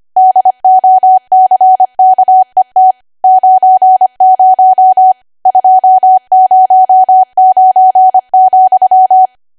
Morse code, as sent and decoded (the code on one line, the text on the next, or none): -.. --- -.-. -.- . - / ----. ----- / ..--- ----- ----. --..--
DOCKET 90 209,